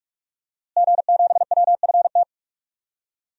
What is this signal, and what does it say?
Morse code, heard as a keyed tone, about 45 wpm, 700 Hz. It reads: G7WFT
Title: DXCC-100 Call Signs - 5 Characters - Repeated 2x - 45wpm